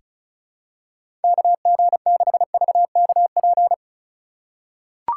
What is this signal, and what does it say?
Morse code, heard as a keyed tone, about 36 wpm, 700 Hz. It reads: KG6VKP
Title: USA Call Signs - 6 Characters - 3x Character Spacing - 35wpm